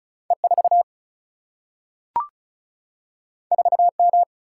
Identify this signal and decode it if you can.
Morse code, heard as a keyed tone, about 36 wpm, 700 Hz. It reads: E4 4M
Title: Single-Letter Single-Number 35wpm